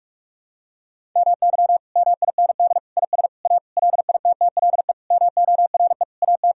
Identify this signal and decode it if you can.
Morse code, heard as a keyed tone, about 44 wpm, 700 Hz. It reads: MY MIND IS A LITTLE MORE AT
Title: Sentences from Top 1000 Words - Review Part 1 - 45wpm